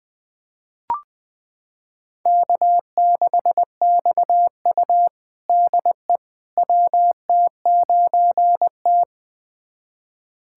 Morse code, heard as a keyed tone, about 21 wpm, 700 Hz. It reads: K6XU DE WT9T
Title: USA Callsign DE Callsign - 4 Characters - 20wpm